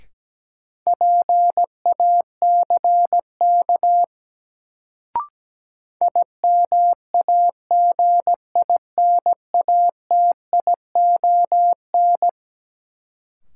Morse code, heard as a keyed tone, about 17 wpm, 700 Hz. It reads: PACK IMAGINATION
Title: Sentences from Top 1300 Words - New Word Drill Alt - 17wpm